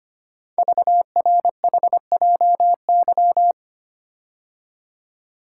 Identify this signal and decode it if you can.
Morse code, heard as a keyed tone, about 25 wpm, 700 Hz. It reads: VRHJY
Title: Code Groups - 5 Letter:Numbers - 25wpm